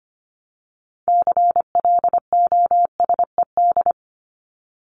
Morse code, heard as a keyed tone, about 25 wpm, 700 Hz. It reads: CLOSED
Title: ED Suffix - Top 100 Words with Base Words 25wpm